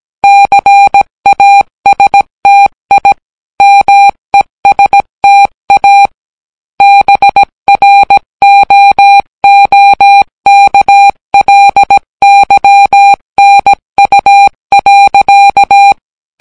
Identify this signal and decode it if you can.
Morse code, heard as a keyed tone, about 17 wpm, 800 Hz. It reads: CASTI MESTA BROOKLYNU.